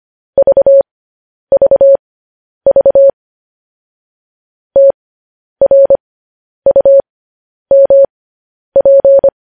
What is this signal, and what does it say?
Morse code, heard as a keyed tone, about 25 wpm, spaced as for 10 wpm, 550 Hz. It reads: VVV TRUMP
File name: podcast / News headlines-1x-25x10